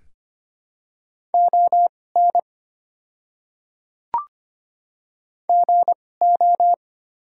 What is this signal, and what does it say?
Morse code, heard as a keyed tone, about 25 wpm, spaced as for 18 wpm, 700 Hz. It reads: ON GO
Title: 2x Character Spacing - Top 300 Words - 25wpm